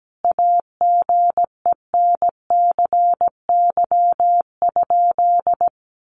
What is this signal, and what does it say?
Morse code, heard as a keyed tone, about 17 wpm, 700 Hz. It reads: AGENCY?
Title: Sentences from Top 1300 Words 17wpm